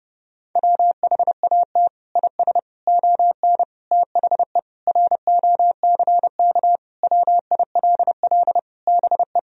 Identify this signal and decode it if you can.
Morse code, heard as a keyed tone, about 30 wpm, 700 Hz. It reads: WHAT IS ON THE ROCK WILL BE
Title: Sentences from Top 1000 Words - Review Part 2 - 30wpm